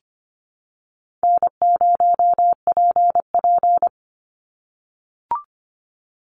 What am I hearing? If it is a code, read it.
Morse code, heard as a keyed tone, about 25 wpm, 700 Hz. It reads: N0PP
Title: Easy Call Signs - 25wpm